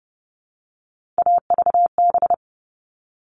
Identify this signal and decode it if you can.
Morse code, heard as a keyed tone, about 29 wpm, 700 Hz. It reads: AVB
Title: All Cut Numbers - 3-Digit 30wpm